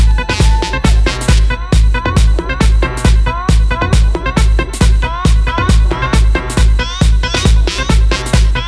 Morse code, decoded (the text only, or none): none